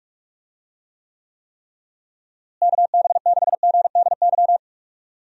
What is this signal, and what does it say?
Morse code, heard as a keyed tone, about 43 wpm, 700 Hz. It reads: KB6GDY
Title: USA Call Signs - 6 Characters - Repeated 3x - 45wpm